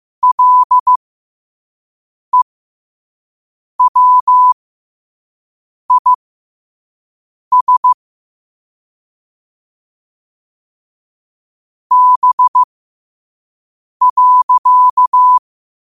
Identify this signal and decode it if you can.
Morse code, heard as a keyed tone, about 15 wpm, spaced as for 5 wpm, 1000 Hz. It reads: LEWIS B.
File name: qotdfiles / 5wpm